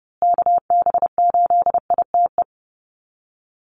Morse code, heard as a keyed tone, about 28 wpm, 700 Hz. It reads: KB8ITE
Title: USA Call Signs Repeated 2x 30wpm